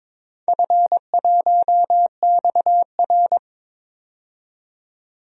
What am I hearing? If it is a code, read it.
Morse code, heard as a keyed tone, about 22 wpm, 700 Hz. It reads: F1XR
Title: DXCC-100 Call Signs - 4 Characters - 22wpm